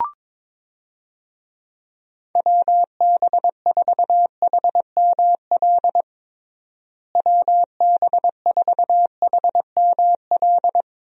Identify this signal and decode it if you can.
Morse code, heard as a keyed tone, about 21 wpm, 700 Hz. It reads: WB4HML WB4HML
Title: USA Call Signs - 6 Characters - Repeated 2x - 22wpm